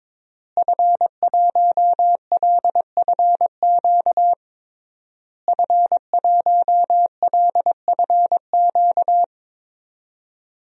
Morse code, heard as a keyed tone, about 21 wpm, 700 Hz. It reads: F1LFQ F1LFQ
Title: DXCC-100 Call Signs - 5 Characters - Repeated 3x - 22wpm